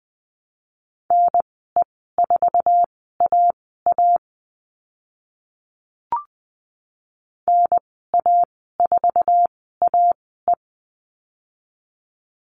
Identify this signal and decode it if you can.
Morse code, heard as a keyed tone, about 20 wpm, spaced as for 14 wpm, 700 Hz. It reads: NE4AA NA4AE
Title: 069 Lesson 12g - Letter d - Call Signs 20wpm